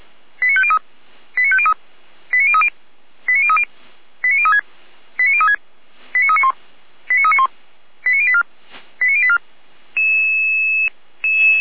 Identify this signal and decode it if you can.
Five-tone selective call (ZVEI), heard as quick runs of five tones, tones of about 70 ms. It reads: 89503 89503 89030 89030 89036 89036 89391 89391 89084 89084